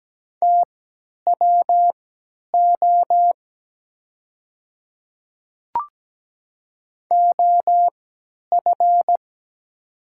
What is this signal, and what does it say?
Morse code, heard as a keyed tone, about 17 wpm, spaced as for 10 wpm, 700 Hz. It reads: TWO OF
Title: 3x Character Spacing - Top 100 Words - 17wpm